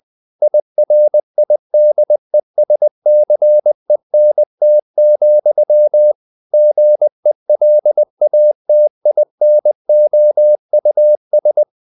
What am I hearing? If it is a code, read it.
Morse code, heard as a keyed tone, about 20 wpm, 600 Hz. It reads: IRIDESCENT, GELATINOUS
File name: the old man and the sea / the old man and the sea part 1 20wpm